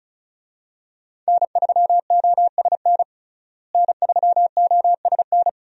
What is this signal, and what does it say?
Morse code, heard as a keyed tone, about 34 wpm, 700 Hz. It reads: N3OSN N3OSN
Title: USA Call Signs - 5 Characters - Repeated 2x - 35wpm